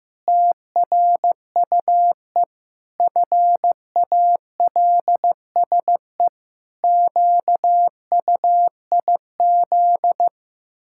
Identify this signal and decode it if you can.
Morse code, heard as a keyed tone, about 16 wpm, 700 Hz. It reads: TRUE FALSE QUIZ
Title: Pangram Challenge v2 - 15wpm